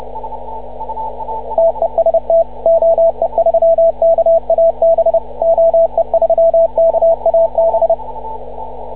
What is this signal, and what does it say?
Morse code, heard as a keyed tone, about 29 wpm, 660 Hz, with radio noise. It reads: TEST OE3KAB OE3KAB